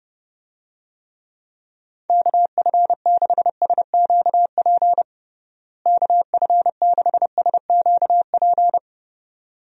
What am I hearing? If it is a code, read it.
Morse code, heard as a keyed tone, about 30 wpm, 700 Hz. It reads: KF6SQP KF6SQP
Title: USA Call Signs - 6 Characters - Repeated 2x - 30wpm